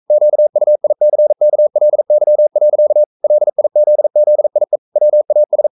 Morse code, heard as a keyed tone, about 42 wpm, 600 Hz. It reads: QUICKLY. LIZZIE WAS
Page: Romanen Never af Ken Follet på engelsk.